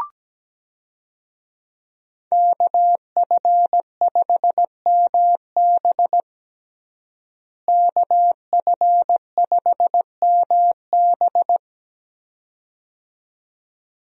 Morse code, heard as a keyed tone, about 17 wpm, 700 Hz. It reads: KF5MB KF5MB
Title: USA Call Signs - 5 Characters - Repeated 2x - 17wpm